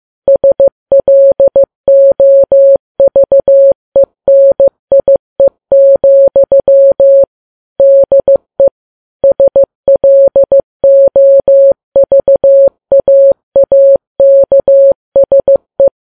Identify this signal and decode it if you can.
Morse code, heard as a keyed tone, about 15 wpm, 550 Hz. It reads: SLOVENIE, DE SLOVAAKSE
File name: Files Web tls / Morse on9fv / morse15kps